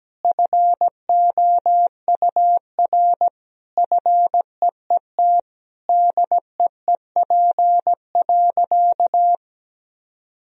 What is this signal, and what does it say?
Morse code, heard as a keyed tone, about 17 wpm, 700 Hz. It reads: FOUR FEET DEEP.
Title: Sentences from Top 1000 Words - Review Part 3 - 17wpm